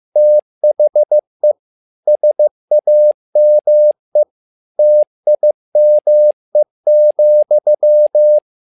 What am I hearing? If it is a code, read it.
Morse code, heard as a keyed tone, about 15 wpm, 600 Hz. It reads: THE SAME TIME,
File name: learning center / tower trouble / tower trouble 15wpm